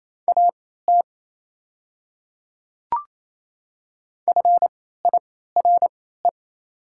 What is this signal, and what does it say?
Morse code, heard as a keyed tone, about 28 wpm, spaced as for 16 wpm, 700 Hz. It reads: AT FIRE
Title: 3x Character Spacing - Top 500 Words - 28wpm